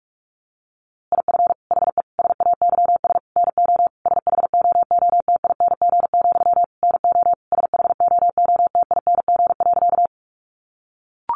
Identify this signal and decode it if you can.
Morse code, heard as a keyed tone, about 44 wpm, 700 Hz. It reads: IF HE SAYS NO SHOOTING, NO SHOOTING.
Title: A Sound Of Thunder 45wpm